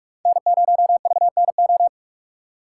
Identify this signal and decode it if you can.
Morse code, heard as a keyed tone, about 43 wpm, 700 Hz. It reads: N0VNO